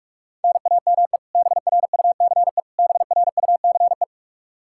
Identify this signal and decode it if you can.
Morse code, heard as a keyed tone, about 46 wpm, 700 Hz. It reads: NAME BRUCE BRUCE
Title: Practice QSO - Exchange 1 - Beginning - 45wpm